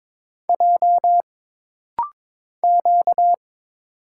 Morse code, heard as a keyed tone, about 22 wpm, 700 Hz. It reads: J Q
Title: Single Letters - Rapid-Fire 22wpm